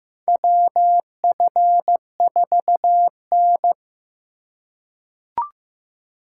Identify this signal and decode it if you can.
Morse code, heard as a keyed tone, about 15 wpm, 700 Hz. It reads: WF4N
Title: USA Call Signs - 4 Characters - 2x Character Spacing - 15wpm